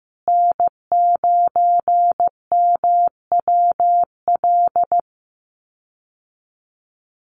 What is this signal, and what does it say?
Morse code, heard as a keyed tone, about 15 wpm, 700 Hz. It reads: N9MWL
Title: USA Call Signs Repeated 2x 15wpm